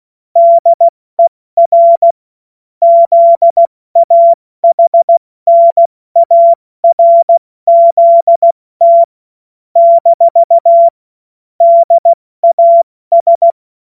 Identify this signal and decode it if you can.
Morse code, heard as a keyed tone, about 16 wpm, 680 Hz. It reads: DER ZAHNARZT - DAS
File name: Morsekurs / Witze / Witz 25. Juli 2023 70 F80